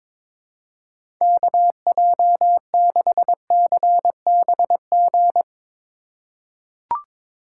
Morse code, heard as a keyed tone, about 22 wpm, 700 Hz. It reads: KJ6CBG
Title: USA Call Signs - 2x Character Spacing - 22wpm